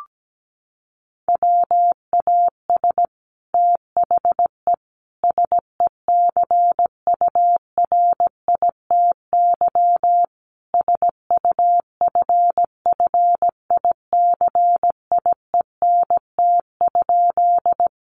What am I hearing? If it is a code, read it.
Morse code, heard as a keyed tone, about 17 wpm, 700 Hz. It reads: WAS THE SECURITY SUFFICIENT?
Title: Sentences from Top 1400 Words 17wpm